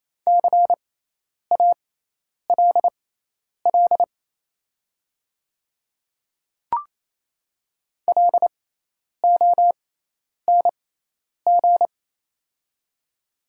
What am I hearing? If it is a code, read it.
Morse code, heard as a keyed tone, about 28 wpm, spaced as for 10 wpm, 690 Hz. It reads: CALL LONG
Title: 6x Character Spacing - Top 100 Words - 28wpm